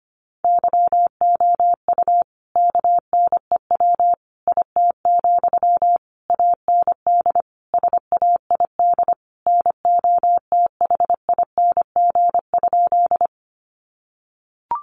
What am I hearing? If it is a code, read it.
Morse code, heard as a keyed tone, about 25 wpm, 700 Hz. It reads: YOU KNEW IT, AND SAID NOTHING?
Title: Sentences from Top 500 Words 25wpm